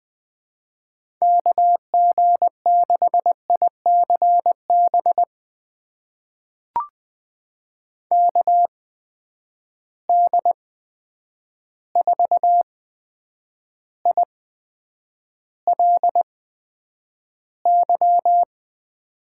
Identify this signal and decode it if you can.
Morse code, heard as a keyed tone, about 20 wpm, spaced as for 9 wpm, 700 Hz. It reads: KG6ICB KD4ILY
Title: USA Call Signs - 6 Characters - 8x Character Spacing - 20wpm